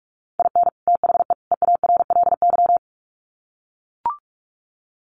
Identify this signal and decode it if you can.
Morse code, heard as a keyed tone, about 44 wpm, 700 Hz. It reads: IN THE EARLY
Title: Sets of 3 Words 45wpm